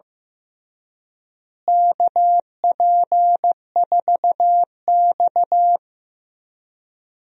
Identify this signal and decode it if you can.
Morse code, heard as a keyed tone, about 15 wpm, 700 Hz. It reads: KP4X
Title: DXCC-100 Call Signs - 4 Characters - Repeated 3x - 15wpm